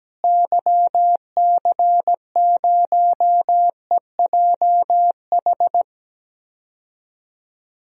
Morse code, heard as a keyed tone, about 17 wpm, 700 Hz. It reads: YC0EJH
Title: DXCC-100 Call Signs - 6 Characters - Repeated 3x - 17wpm